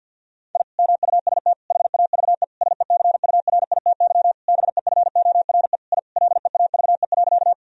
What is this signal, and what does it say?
Morse code, heard as a keyed tone, about 50 wpm, 700 Hz. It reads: I MUST HAVE SECURITY BEFORE I LEAVE.
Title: Sentences from Top 1000 Words - Review Part 4 - 50wpm